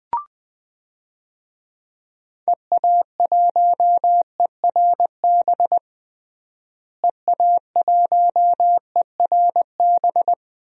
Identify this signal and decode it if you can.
Morse code, heard as a keyed tone, about 20 wpm, 700 Hz. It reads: EA1ERB EA1ERB
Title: DXCC-100 Call Signs - 6 Characters - Repeated 3x - 20wpm